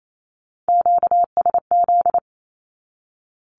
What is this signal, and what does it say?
Morse code, heard as a keyed tone, about 28 wpm, 700 Hz. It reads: QSZ
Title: Q-Codes 28wpm